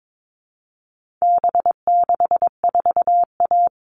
Morse code, heard as a keyed tone, about 22 wpm, 700 Hz. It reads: B64A
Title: All Cut Numbers - 4-Digit 22wpm